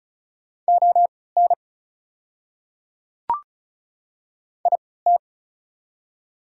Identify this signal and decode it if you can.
Morse code, heard as a keyed tone, about 35 wpm, spaced as for 20 wpm, 700 Hz. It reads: ON IT
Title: 3x Character Spacing - Top 100 Words - 35wpm